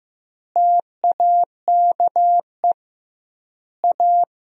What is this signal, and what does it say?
Morse code, heard as a keyed tone, about 16 wpm, 700 Hz. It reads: TAKE A
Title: 2x Word-Spacing - Sentences from Top 200 Words - 15wpm